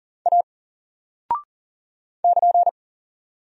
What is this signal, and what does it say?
Morse code, heard as a keyed tone, about 39 wpm, 700 Hz. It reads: A <KN>
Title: Single Character (Letter:Number:Punctuation:Prosign) 40wpm